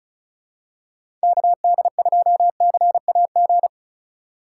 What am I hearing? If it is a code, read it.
Morse code, heard as a keyed tone, about 36 wpm, 700 Hz. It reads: KD2CAG
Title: USA Call Signs - 6 Characters - 2x Character Spacing - 35wpm